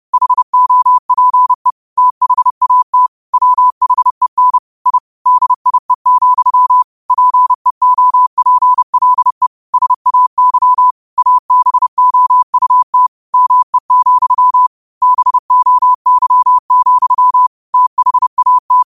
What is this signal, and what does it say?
Morse code, heard as a keyed tone, about 30 wpm, 1000 Hz. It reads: HOPE THAT WHEN I DIE, PEOPLE SAY ABOUT ME, BOY, THAT